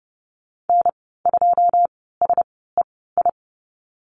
Morse code, heard as a keyed tone, about 30 wpm, spaced as for 17 wpm, 700 Hz. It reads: N2SEI
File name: Lesson 14g - Number 2 - Call Signs 30wpm